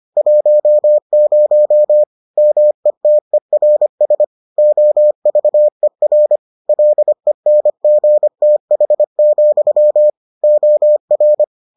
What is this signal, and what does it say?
Morse code, heard as a keyed tone, about 25 wpm, 600 Hz. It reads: 10 METERS OVER LENGTH, OR